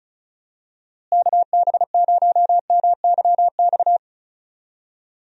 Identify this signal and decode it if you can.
Morse code, heard as a keyed tone, about 34 wpm, 700 Hz. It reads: KB0MYX
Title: USA Call Signs - 6 Characters - Repeated 3x - 35wpm